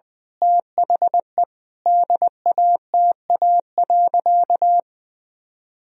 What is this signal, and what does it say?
Morse code, heard as a keyed tone, about 20 wpm, 700 Hz. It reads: THE DATA.
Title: Sentences from Top 1000 Words - Review Part 3 - 20wpm